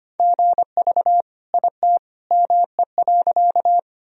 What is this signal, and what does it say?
Morse code, heard as a keyed tone, about 25 wpm, 700 Hz. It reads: GV IT ME.
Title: Sentences from Top 200 Words and Abbreviations 25wpm